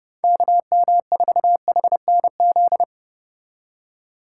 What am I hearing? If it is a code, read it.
Morse code, heard as a keyed tone, about 32 wpm, 700 Hz. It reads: KM4HNZ